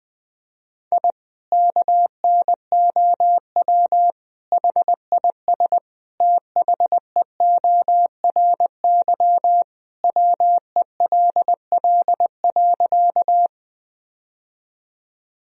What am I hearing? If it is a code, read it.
Morse code, heard as a keyed tone, about 20 wpm, 700 Hz. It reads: I KNOW HIS THEORY WELL.
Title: Sentences from Top 1000 Words - Review Part 3 - 20wpm